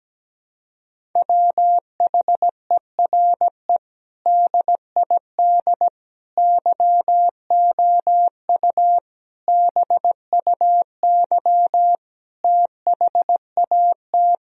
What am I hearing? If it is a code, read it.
Morse code, heard as a keyed tone, about 17 wpm, 700 Hz. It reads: WHERE DID YOU BUY THAT
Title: Sentences from Top 1000 Words - Review Part 4 - 17wpm